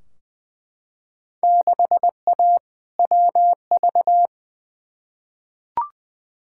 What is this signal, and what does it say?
Morse code, heard as a keyed tone, about 20 wpm, 700 Hz. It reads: 6A WV
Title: Field Day Exchanges - 20wpm